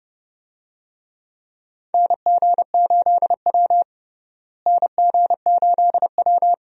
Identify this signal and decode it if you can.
Morse code, heard as a keyed tone, about 30 wpm, 700 Hz. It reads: NG8W NG8W
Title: USA Call Signs - 4 Characters - Repeated 2x - 30wpm